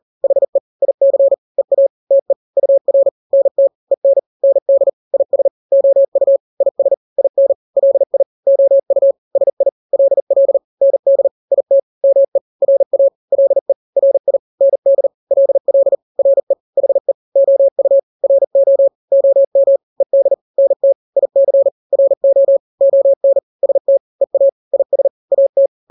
Morse code, heard as a keyed tone, about 40 wpm, 550 Hz. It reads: HE IC EA TE UR NT EN ND IS OU IS IN LI OU SI LL ND IT ME RA LE RI ND LL RE HE OU RO OM ED NT IC RO ON ST EA IS AT
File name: n-grams / 2-grams 40 35 WPM